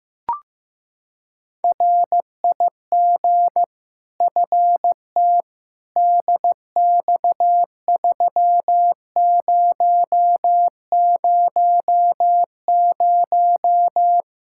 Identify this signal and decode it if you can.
Morse code, heard as a keyed tone, about 15 wpm, 700 Hz. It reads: RIG FT DX3000
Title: Practice QSO - Exchange 2 - Info - 15wpm